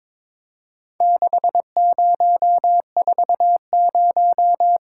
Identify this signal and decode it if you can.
Morse code, heard as a keyed tone, about 23 wpm, 700 Hz. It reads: 6040
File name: practice / 22wpm / Four Letter-Number Combinations - 22wpm